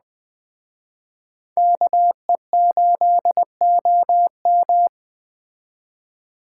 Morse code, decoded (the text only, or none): KE8OM